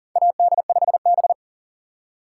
Morse code, heard as a keyed tone, about 40 wpm, 700 Hz. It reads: AD5B
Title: USA Call Signs - 4 Characters - 3x Character Spacing - 40wpm